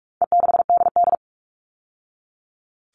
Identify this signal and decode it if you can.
Morse code, heard as a keyed tone, about 44 wpm, 700 Hz. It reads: E6DD
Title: All Cut Numbers - 4-Digit 45wpm